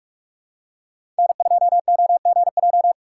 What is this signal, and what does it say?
Morse code, heard as a keyed tone, about 47 wpm, 700 Hz. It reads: N2OGJ
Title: USA Call Signs - 5 Characters - 8x Character Spacing - 45wpm